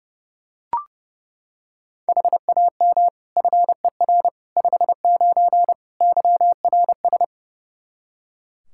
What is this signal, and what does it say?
Morse code, heard as a keyed tone, about 32 wpm, 700 Hz. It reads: HAM FER 59 YRS
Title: Practice QSO - Exchange 3 - More Info - 30wpm